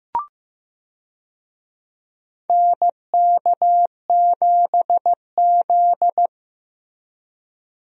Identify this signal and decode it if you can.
Morse code, heard as a keyed tone, about 15 wpm, 700 Hz. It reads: NK7Z
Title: USA Call Signs - 4 Characters - Repeated 3x - 15wpm